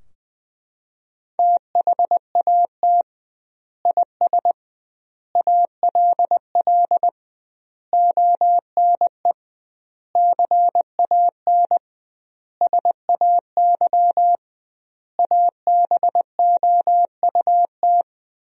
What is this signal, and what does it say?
Morse code, heard as a keyed tone, about 20 wpm, 700 Hz. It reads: THAT IS ALL ONE CAN SAY ABOUT
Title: 2x Word-Spacing - Sentences from Top 200 Words - 20wpm